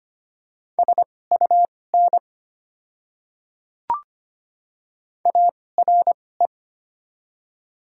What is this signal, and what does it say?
Morse code, heard as a keyed tone, about 25 wpm, spaced as for 18 wpm, 700 Hz. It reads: SUN ARE
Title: 2x Character Spacing - Top 500 Words - 25wpm